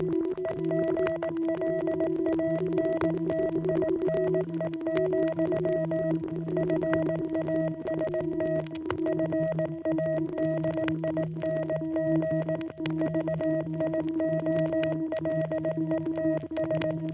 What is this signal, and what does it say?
Morse code, heard as a keyed tone, about 18 wpm, 660 Hz. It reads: EGLANTINE, FAST FADING VIOLETS